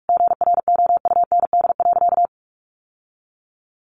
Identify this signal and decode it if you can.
Morse code, heard as a keyed tone, about 44 wpm, 700 Hz. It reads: GROUND.
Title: Sentences from Top 1500 Words 45wpm